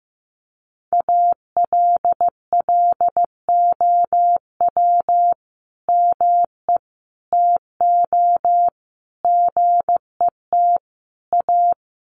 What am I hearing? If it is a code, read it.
Morse code, heard as a keyed tone, about 14 wpm, 700 Hz. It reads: ALLOW ME TO GET A
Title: Sentences from Top 1300 Words 15wpm